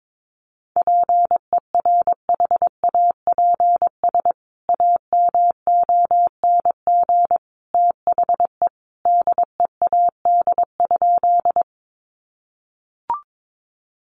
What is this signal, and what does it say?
Morse code, heard as a keyed tone, about 22 wpm, 700 Hz. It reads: PERHAPS AMONG THE DEAD?
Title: Sentences from Top 1500 Words 22wpm